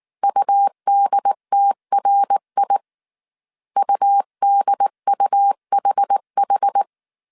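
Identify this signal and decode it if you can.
Morse code, heard as a keyed tone, about 19 wpm, 800 Hz. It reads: UDTRI UDUHH
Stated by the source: Morse code dictation at 18 wpm from the IW3SGT QRS Morse code training course, five-character groups of 300 characters.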